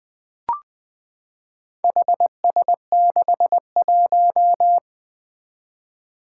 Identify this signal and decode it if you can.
Morse code, heard as a keyed tone, about 20 wpm, 700 Hz. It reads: HS61
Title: Four Challenging Letter-Number Combinations 20wpm